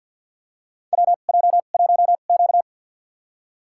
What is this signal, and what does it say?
Morse code, heard as a keyed tone, about 51 wpm, spaced as for 36 wpm, 700 Hz. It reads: WJ1Q
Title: USA Call Signs - 2x Character Spacing - 50wpm